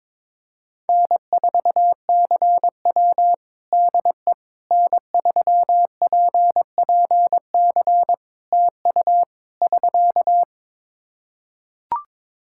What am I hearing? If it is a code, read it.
Morse code, heard as a keyed tone, about 21 wpm, 700 Hz. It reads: N4CW DE N3PPC TU <SK>
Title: Practice QSO - Exchange 4 - Ending - 22wpm